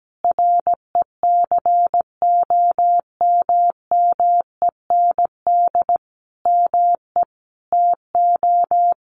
Morse code, heard as a keyed tone, about 17 wpm, 700 Hz. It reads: RECOMMEND ME TO
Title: Sentences from Top 1200 Words 17wpm